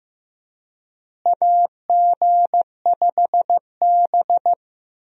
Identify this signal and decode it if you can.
Morse code, heard as a keyed tone, about 16 wpm, 700 Hz. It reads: AG5B
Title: USA Call Signs - 4 Characters - Repeated 2x - 15wpm